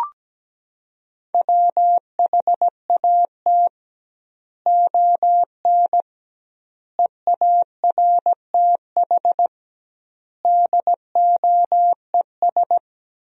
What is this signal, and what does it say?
Morse code, heard as a keyed tone, about 17 wpm, 700 Hz. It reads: WHAT ON EARTH DOES
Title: 2x Word-Spacing - Sentences from Top 200 Words - 17wpm